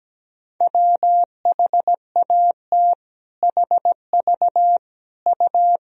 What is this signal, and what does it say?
Morse code, heard as a keyed tone, about 17 wpm, 700 Hz. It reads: WHAT HV U
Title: Sentences from Top 100 Words and Abbreviations 17wpm